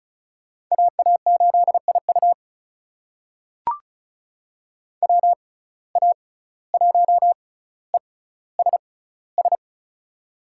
Morse code, heard as a keyed tone, about 36 wpm, spaced as for 17 wpm, 700 Hz. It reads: AA8IU WA1ESS
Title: USA Call Signs - 6x Character Spacing - 35wpm